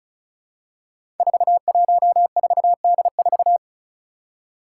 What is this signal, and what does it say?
Morse code, heard as a keyed tone, about 36 wpm, 700 Hz. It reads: 414D4
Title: Five Challenging Letter-Number Combinations 35wpm